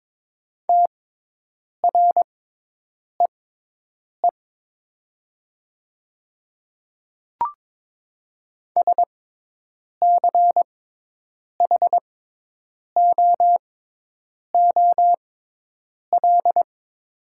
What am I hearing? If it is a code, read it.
Morse code, heard as a keyed tone, about 22 wpm, spaced as for 8 wpm, 700 Hz. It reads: TREE SCHOOL
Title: 6x Character Spacing - Top 300 Words - 22wpm